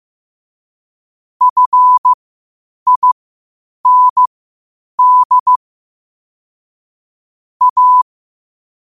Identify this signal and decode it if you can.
Morse code, heard as a keyed tone, about 15 wpm, spaced as for 8 wpm, 1000 Hz. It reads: FIND A
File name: qotdfiles / 8wpm